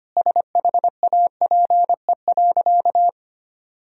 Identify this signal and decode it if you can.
Morse code, heard as a keyed tone, about 25 wpm, 700 Hz. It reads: SHAPE.
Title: Sentences from Top 1000 Words - Review Part 3 - 25wpm